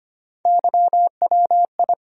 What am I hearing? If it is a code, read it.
Morse code, heard as a keyed tone, about 25 wpm, 700 Hz. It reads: YWI
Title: USA Call Sign Postfixes - 3 Letters 25wpm